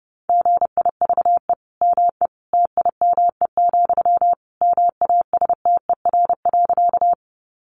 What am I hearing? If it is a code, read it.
Morse code, heard as a keyed tone, about 30 wpm, 700 Hz. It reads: GIVE ME TIME, MASTER.
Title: Sentences from Top 1300 Words 30wpm